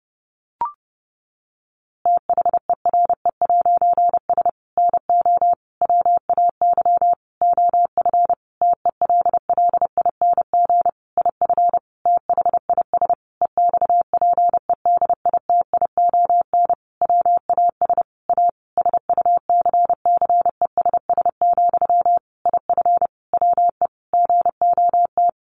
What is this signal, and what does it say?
Morse code, heard as a keyed tone, about 30 wpm, 700 Hz. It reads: THERE'S NO WAY OF TELLING IF THIS EXPEDITION WAS A SUCCESS, IF WE GOT
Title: A Sound Of Thunder 30wpm